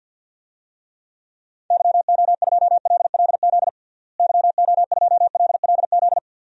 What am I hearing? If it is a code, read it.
Morse code, heard as a keyed tone, about 47 wpm, 700 Hz. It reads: YO2LLZ YO2LLZ
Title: DXCC-100 Call Signs - 6 Characters - Repeated 2x - 50wpm